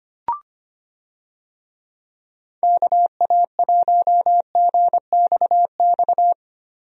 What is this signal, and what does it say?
Morse code, heard as a keyed tone, about 25 wpm, 700 Hz. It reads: KA1GXX
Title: USA Call Signs - 6 Characters - Repeated 2x - 25wpm